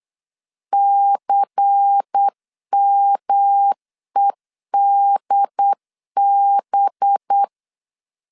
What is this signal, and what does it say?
Morse code, heard as a keyed tone, about 8 wpm, 800 Hz. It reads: CMEDB
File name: CW / DET / 1 9 1500C S C 8wpm 5653